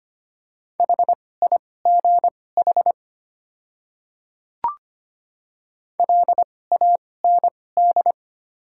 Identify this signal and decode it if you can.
Morse code, heard as a keyed tone, about 25 wpm, spaced as for 18 wpm, 700 Hz. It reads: HIGH LAND
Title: 2x Character Spacing - Top 500 Words - 25wpm